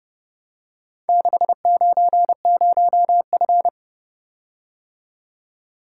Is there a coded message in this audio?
Yes: Morse code from 1.1 to 3.7 s, heard as a keyed tone, about 30 wpm, 700 Hz. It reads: 690F